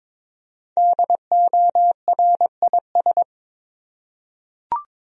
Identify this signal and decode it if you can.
Morse code, heard as a keyed tone, about 22 wpm, 700 Hz. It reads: DORIS
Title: USA Common Names - 22wpm